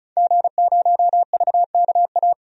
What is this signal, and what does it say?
Morse code, heard as a keyed tone, about 34 wpm, 700 Hz. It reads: G0VKA
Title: DXCC-100 Call Signs - 5 Characters - 35wpm